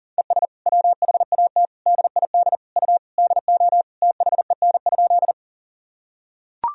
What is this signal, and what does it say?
Morse code, heard as a keyed tone, about 40 wpm, 700 Hz. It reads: ES WHAT DID U DO THEN?
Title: Sentences from Top 100 Words and Abbreviations 40wpm